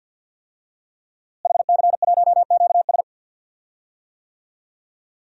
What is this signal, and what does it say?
Morse code, heard as a keyed tone, about 50 wpm, 700 Hz. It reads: HC1QS
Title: DXCC-100 Call Signs - 5 Characters - Repeated 3x - 50wpm